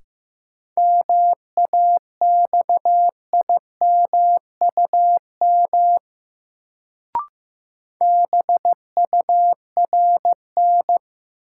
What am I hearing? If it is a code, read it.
Morse code, heard as a keyed tone, about 15 wpm, 700 Hz. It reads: MAXIMUM BURN
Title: Sentences from Top 1300 Words - New Word Drill - 15wpm